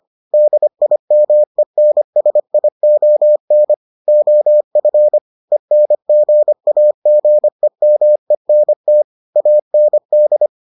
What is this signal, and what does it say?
Morse code, heard as a keyed tone, about 25 wpm, 600 Hz. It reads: DIMENSION OF ENGAGEMENT AND